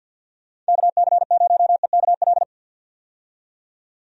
Morse code, heard as a keyed tone, about 50 wpm, 700 Hz. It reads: KC0EKL
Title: USA Call Signs - 6 Characters - 50wpm